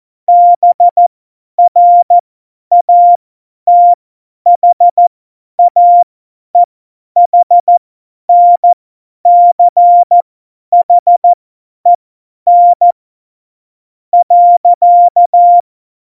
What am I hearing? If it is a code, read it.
Morse code, heard as a keyed tone, about 14 wpm, spaced as for 9 wpm, 700 Hz. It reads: BRATHAEHNCHEN.
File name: Morsekurs / Witze / Witz 02. April 2019, 50 F70